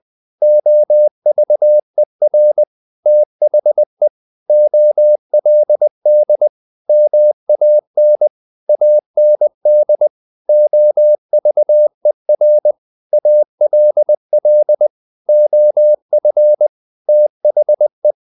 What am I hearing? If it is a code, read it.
Morse code, heard as a keyed tone, about 20 wpm, 600 Hz. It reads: OVER THE OLD MAN AND OVER ALL OF THE
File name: the old man and the sea part 2 20wpm